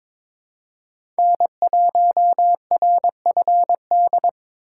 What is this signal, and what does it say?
Morse code, heard as a keyed tone, about 26 wpm, 700 Hz. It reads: N1RFD
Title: Speed Racing - USA Call Signs 22wpm